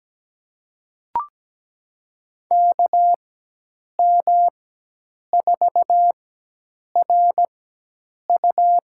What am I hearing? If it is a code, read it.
Morse code, heard as a keyed tone, about 17 wpm, spaced as for 8 wpm, 700 Hz. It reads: KM4RU
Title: USA Call Signs - 6 Characters - 4x Character Spacing - 17wpm